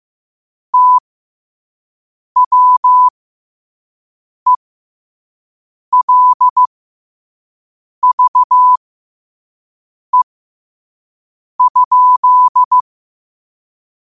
Morse code, heard as a keyed tone, about 15 wpm, spaced as for 5 wpm, 1000 Hz. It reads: TWELVE?
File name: qotdfiles / 5wpm